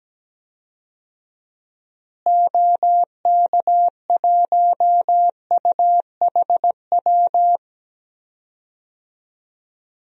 Morse code, heard as a keyed tone, about 17 wpm, 700 Hz. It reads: OK1UHW
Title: DXCC-100 Call Signs - 6 Characters - Repeated 3x - 17wpm